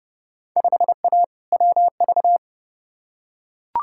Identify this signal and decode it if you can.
Morse code, heard as a keyed tone, about 29 wpm, 700 Hz. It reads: 5A WV
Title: Field Day Exchanges - 30wpm